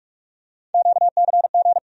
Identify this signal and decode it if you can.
Morse code, heard as a keyed tone, about 43 wpm, 700 Hz. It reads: QCG
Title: USA Call Sign Postfixes - 3 Letters 45wpm